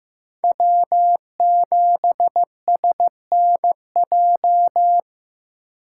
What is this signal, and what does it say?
Morse code, heard as a keyed tone, about 15 wpm, 700 Hz. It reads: W7SNJ
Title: USA Call Signs - 5 Characters - 4x Character Spacing - 15wpm